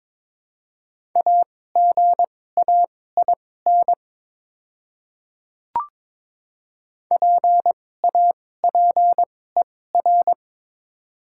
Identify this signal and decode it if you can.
Morse code, heard as a keyed tone, about 22 wpm, spaced as for 16 wpm, 700 Hz. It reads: AGAIN PAPER
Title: 2x Character Spacing - Top 500 Words - 22wpm